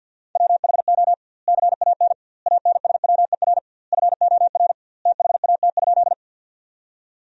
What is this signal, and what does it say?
Morse code, heard as a keyed tone, about 50 wpm, 700 Hz. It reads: WHO CAN ANSWER FOR THAT?
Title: 2x Word-Spacing - Sentences from Top 300 Words - 50wpm